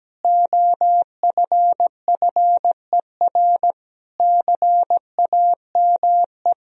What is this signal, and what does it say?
Morse code, heard as a keyed tone, about 17 wpm, 700 Hz. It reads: OFFER CAME
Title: Sentences from Top 1000 Words - Review Part 4 - 17wpm